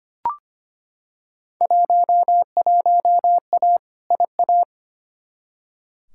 Morse code, heard as a keyed tone, about 25 wpm, 700 Hz. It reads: 11A IA
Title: Field Day Exchanges Encore - 25wpm